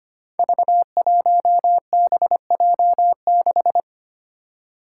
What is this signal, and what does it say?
Morse code, heard as a keyed tone, about 25 wpm, 700 Hz. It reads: V1BJ6
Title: Five Challenging Letter-Number Combinations 25wpm